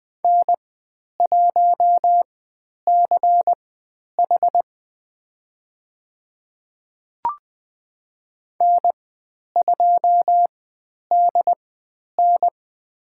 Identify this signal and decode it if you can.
Morse code, heard as a keyed tone, about 20 wpm, spaced as for 10 wpm, 700 Hz. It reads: N1CH N2DN
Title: 095 Lesson 16g - Letter c - Call Signs 20wpm